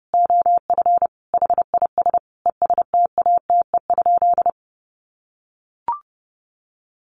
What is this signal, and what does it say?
Morse code, heard as a keyed tone, about 30 wpm, 700 Hz. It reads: OF HIS ESTATE?
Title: Sentences from Top 1200 Words 30wpm